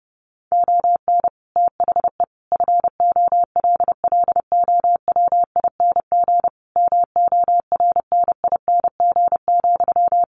Morse code, heard as a keyed tone, about 30 wpm, 700 Hz. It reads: ON THE FOLLOWING MORNING,
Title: Sentences from Top 1400 Words 30wpm